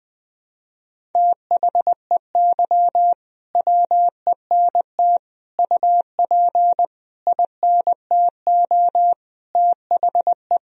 Morse code, heard as a keyed tone, about 20 wpm, 700 Hz. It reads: THEY WENT UP INTO THE
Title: Sentences from Top 1000 Words - Review Part 1 - 20wpm